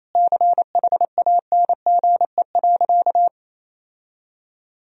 Morse code, heard as a keyed tone, about 28 wpm, 700 Hz. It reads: CHANGE.
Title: Sentences from Top 1000 Words - Review Part 3 - 28wpm